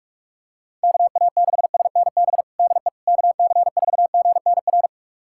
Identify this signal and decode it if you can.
Morse code, heard as a keyed tone, about 46 wpm, 700 Hz. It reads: KA6SNB DE KC4GNR
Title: USA Callsign DE Callsign - 6 Characters - 45wpm